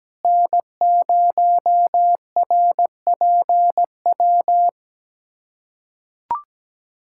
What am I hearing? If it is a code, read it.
Morse code, heard as a keyed tone, about 17 wpm, 700 Hz. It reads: N0RPW
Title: USA Callsign DE Callsign - 5 Characters - 17wpm